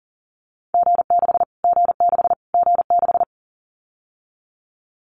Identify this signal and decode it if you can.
Morse code, heard as a keyed tone, about 40 wpm, 700 Hz. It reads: G6 G6 G6
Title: Callsign Prefixes for Top 25 DXCC - Repeated 3x - 40wpm